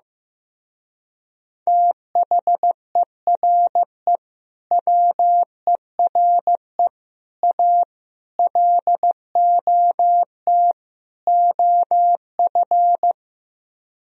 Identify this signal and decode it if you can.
Morse code, heard as a keyed tone, about 16 wpm, 700 Hz. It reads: THERE WERE A LOT OF
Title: Speed-Racing - Sets of 5 Words 15wpm